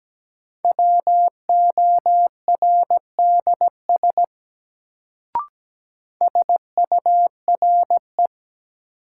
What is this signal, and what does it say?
Morse code, heard as a keyed tone, about 17 wpm, 700 Hz. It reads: WORDS SURE
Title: Sets of 3 Words - New Word Drill - 17wpm